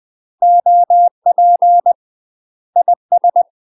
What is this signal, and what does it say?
Morse code, heard as a keyed tone, about 20 wpm, 700 Hz. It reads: OP IS
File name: UK MASTER QSOs TRAINING / 20WPM / UK MASTER QSOs 0043